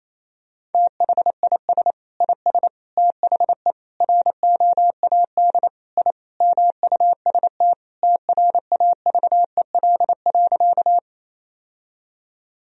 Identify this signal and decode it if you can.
Morse code, heard as a keyed tone, about 28 wpm, 700 Hz. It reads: THIS IS THE ROAD I MUST TRAVEL.
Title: Sentences from Top 1000 Words - Review Part 2 - 28wpm